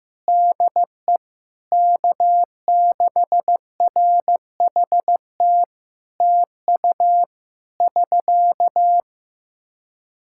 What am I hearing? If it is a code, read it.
Morse code, heard as a keyed tone, about 15 wpm, 700 Hz. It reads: DE K6RHT TU <SK>
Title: Practice QSO - Exchange 4 - Ending - 15wpm